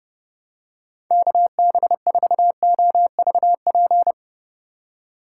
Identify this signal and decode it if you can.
Morse code, heard as a keyed tone, about 30 wpm, 700 Hz. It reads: KB4OVP
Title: USA Call Signs - 2x Character Spacing - 30wpm